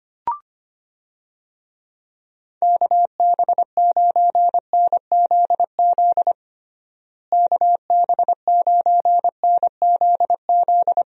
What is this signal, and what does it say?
Morse code, heard as a keyed tone, about 25 wpm, 700 Hz. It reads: KB9NZZ KB9NZZ
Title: USA Call Signs - 6 Characters - Repeated 2x - 25wpm